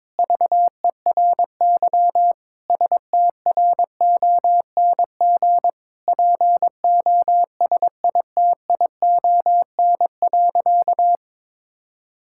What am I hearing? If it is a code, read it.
Morse code, heard as a keyed tone, about 22 wpm, 700 Hz. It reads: VERY STRONG POSITION.
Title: Sentences from Top 1000 Words - Review Part 5 - 22wpm